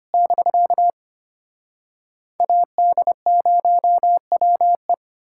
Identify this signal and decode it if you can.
Morse code, heard as a keyed tone, about 33 wpm, 700 Hz. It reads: <BK> AD0WE